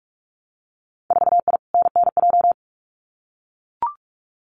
Morse code, heard as a keyed tone, about 44 wpm, 700 Hz. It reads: 4I NNJ
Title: Winter Field Day Exchanges 45wpm